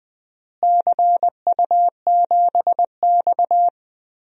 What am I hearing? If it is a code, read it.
Morse code, heard as a keyed tone, about 20 wpm, 700 Hz. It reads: CU7X
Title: DXCC-100 Call Signs - 4 Characters - Repeated 2x - 20wpm